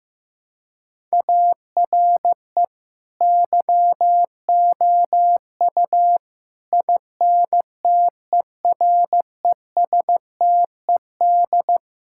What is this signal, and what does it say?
Morse code, heard as a keyed tone, about 15 wpm, 700 Hz. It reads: ARE YOU INTERESTED
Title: Sentences from Top 1000 Words - Review Part 6 - 15wpm